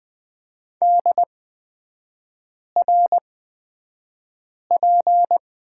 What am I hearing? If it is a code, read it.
Morse code, heard as a keyed tone, about 20 wpm, spaced as for 5 wpm, 700 Hz. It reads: DRP
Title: 224 Lesson 36e - BT - Three Characters 20wpm